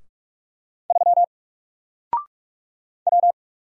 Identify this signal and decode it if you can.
Morse code, heard as a keyed tone, about 46 wpm, 700 Hz. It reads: 3 W
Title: Single Letter-Number 45wpm